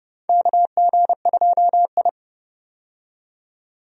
Morse code, heard as a keyed tone, about 29 wpm, 700 Hz. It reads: KG2I
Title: USA Call Signs - 4 Characters - 30wpm